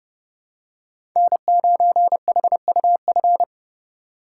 Morse code, heard as a keyed tone, about 30 wpm, 700 Hz. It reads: N9HUF
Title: USA Call Signs - 5 Characters - 4x Character Spacing - 30wpm